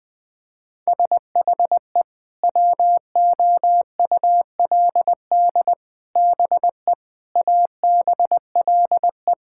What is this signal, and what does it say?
Morse code, heard as a keyed tone, about 20 wpm, 700 Hz. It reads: SHE WOULD BE ABLE
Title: Sentences with 4-Word Phrases - 20wpm